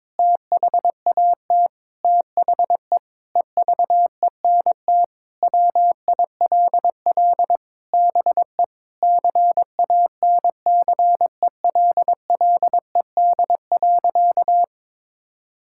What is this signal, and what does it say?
Morse code, heard as a keyed tone, about 22 wpm, 700 Hz. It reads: THAT THE EVENT WILL BE CANCELLED.
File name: practice / 22wpm / Sentences with 8-Word Phrases - 22wpm